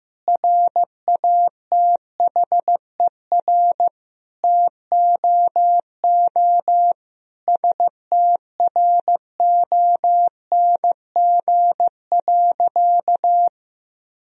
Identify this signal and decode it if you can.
Morse code, heard as a keyed tone, about 15 wpm, 700 Hz. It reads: RATHER TOO STRONG.
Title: Sentences from Top 1000 Words - Review Part 4 - 15wpm